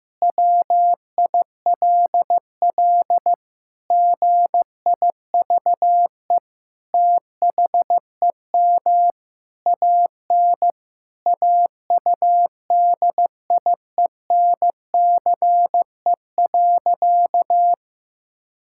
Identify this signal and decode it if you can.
Morse code, heard as a keyed tone, about 15 wpm, 700 Hz. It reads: WILL GIVE THEM AN AUDIENCE.
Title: Sentences from Top 1000 Words - Review Part 5 - 15wpm